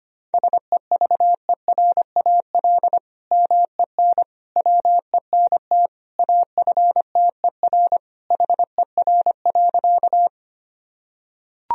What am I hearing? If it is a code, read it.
Morse code, heard as a keyed tone, about 25 wpm, 700 Hz. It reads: SEVERAL MEN WENT AFTER HER.
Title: Sentences from Top 1000 Words - Review Part 2 - 25wpm